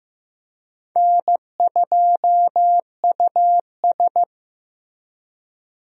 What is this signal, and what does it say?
Morse code, heard as a keyed tone, about 15 wpm, 700 Hz. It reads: N2US
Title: USA Call Signs - 4 Characters - Repeated 3x - 15wpm